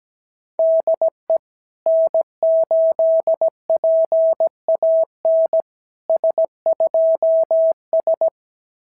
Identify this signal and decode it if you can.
Morse code, heard as a keyed tone, about 18 wpm, 650 Hz. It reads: DE N8PAN S2S
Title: SOTA QSO Practice 17wpm